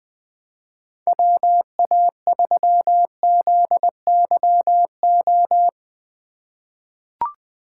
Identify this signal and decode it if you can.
Morse code, heard as a keyed tone, about 20 wpm, 700 Hz. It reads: WA3ZYO
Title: USA Call Signs - 6 Characters - 3x Character Spacing - 20wpm